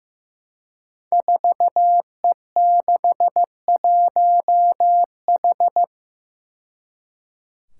Morse code, heard as a keyed tone, about 15 wpm, 700 Hz. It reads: 4E61H
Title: Five Challenging Letter-Number Combinations 15wpm